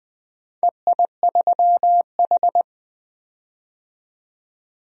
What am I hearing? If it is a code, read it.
Morse code, heard as a keyed tone, about 20 wpm, 700 Hz. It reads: EI3H
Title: DXCC-100 Call Signs - 4 Characters - 20wpm